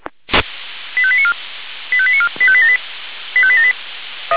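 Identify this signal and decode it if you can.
Five-tone selective call (ZVEI), heard as quick runs of five tones, tones of about 70 ms. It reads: 85894 85894 85878 85878
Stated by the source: ZVEI five-tone selective call 85894 received on SDR.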